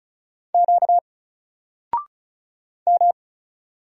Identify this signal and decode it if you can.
Morse code, heard as a keyed tone, about 33 wpm, 700 Hz. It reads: Q M
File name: Single-Letter-Number-35wpm